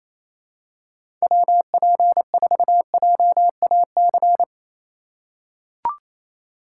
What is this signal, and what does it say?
Morse code, heard as a keyed tone, about 28 wpm, 700 Hz. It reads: WP4JAC
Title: USA Call Signs - 3x Character Spacing - 28wpm